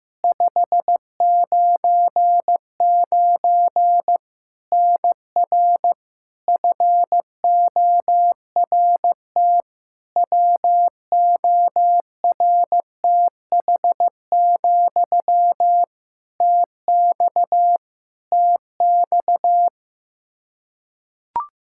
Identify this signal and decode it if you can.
Morse code, heard as a keyed tone, about 15 wpm, 700 Hz. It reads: 599 NR FORT WORTH, TX TX
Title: Practice QSO - Exchange 1 - Beginning - 15wpm